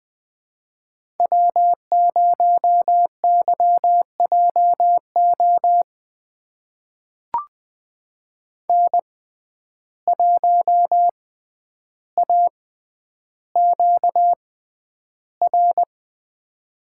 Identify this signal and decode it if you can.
Morse code, heard as a keyed tone, about 20 wpm, spaced as for 10 wpm, 700 Hz. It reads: W0YJO N1AQR
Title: USA Call Signs - 5 Characters - 6x Character Spacing - 20wpm